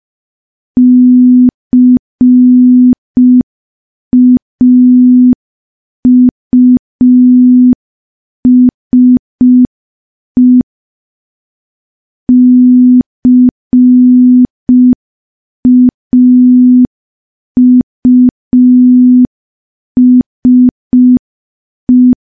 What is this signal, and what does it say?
Morse code, heard as a keyed tone, about 5 wpm, 250 Hz. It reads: CAUSE CAUSE
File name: morse / morse-training-05WPM